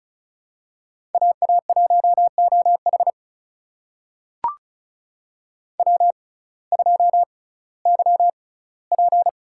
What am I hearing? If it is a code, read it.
Morse code, heard as a keyed tone, about 36 wpm, spaced as for 19 wpm, 700 Hz. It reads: AA1OH W2YP
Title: USA Call Signs - 5 Characters - 6x Character Spacing - 35wpm